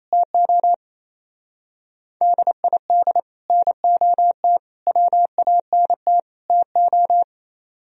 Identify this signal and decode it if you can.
Morse code, heard as a keyed tone, about 33 wpm, 700 Hz. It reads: TO DID NOT WANT TO
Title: Speed-Racing - Sets of 4 Words 28wpm